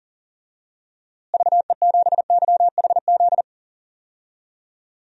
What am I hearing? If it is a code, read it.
Morse code, heard as a keyed tone, about 40 wpm, 700 Hz. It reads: VE7YHZ
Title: DXCC-100 Call Signs - 6 Characters - Repeated 3x - 40wpm